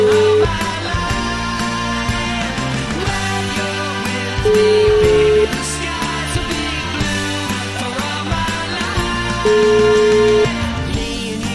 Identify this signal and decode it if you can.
Ringback tone (Europe, 425 Hz), with noise.